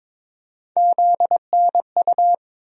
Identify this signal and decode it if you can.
Morse code, heard as a keyed tone, about 23 wpm, 700 Hz. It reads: ZNU
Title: USA Call Sign Postfixes - 3 Letters 22wpm